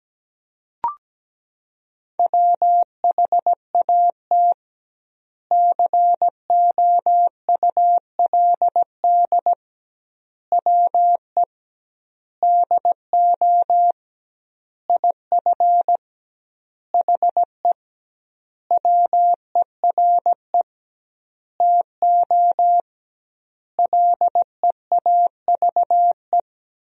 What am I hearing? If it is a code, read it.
Morse code, heard as a keyed tone, about 17 wpm, 700 Hz. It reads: WHAT COULD WE DO IF HE WERE TO LEAVE
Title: 2x Word-Spacing - Sentences from Top 400 Words - 17wpm